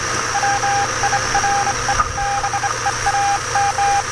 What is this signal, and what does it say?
Morse code, heard as a keyed tone, about 25 wpm, 800 Hz, with noise. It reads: WIRE BEAM
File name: attachments / 20.07.2017 UY1IF RX UA1CEG Sputnik